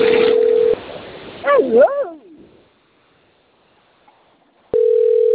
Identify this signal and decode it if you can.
Ringback tone (North America), 440 + 480 Hz.